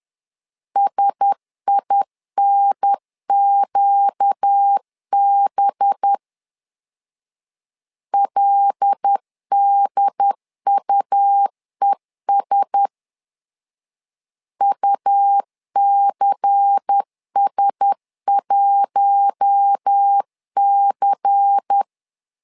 Morse code, heard as a keyed tone, about 10 wpm, 800 Hz. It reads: SINQB LDUES UCS1C
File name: LEZIONI / 5 LBVC / 5 LBVC 10wpm 2349